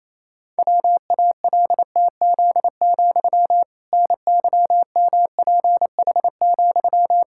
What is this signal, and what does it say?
Morse code, heard as a keyed tone, about 28 wpm, 700 Hz. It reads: WALTZ, NYMPH,
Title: Pangram Challenge v2 - 28wpm